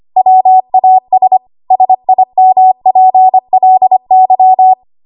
Morse code, heard as a keyed tone, about 25 wpm, 750 Hz. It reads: WAS SIMPLY
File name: Archive / 25 WPM / 25WPM